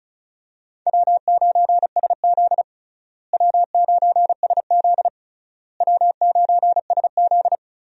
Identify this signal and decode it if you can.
Morse code, heard as a keyed tone, about 36 wpm, 700 Hz. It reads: W9SZ W9SZ W9SZ
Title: USA Call Signs - 4 Characters - Repeated 3x - 35wpm